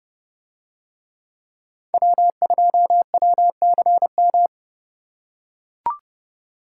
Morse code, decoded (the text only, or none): W2WCM